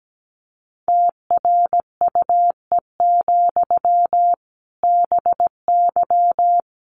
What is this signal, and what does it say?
Morse code, heard as a keyed tone, about 17 wpm, 700 Hz. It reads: TRUE, BY
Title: Sentences from Top 1500 Words 17wpm